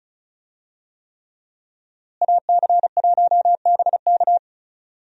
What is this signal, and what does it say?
Morse code, heard as a keyed tone, about 34 wpm, 700 Hz. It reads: AC1BK
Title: USA Call Signs - 5 Characters - Repeated 3x - 35wpm